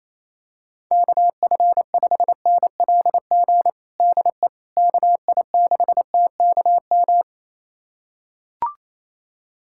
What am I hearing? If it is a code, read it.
Morse code, heard as a keyed tone, about 27 wpm, 700 Hz. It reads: KF5NLG DE KI6TKM
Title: USA Callsign DE Callsign - 6 Characters - 28wpm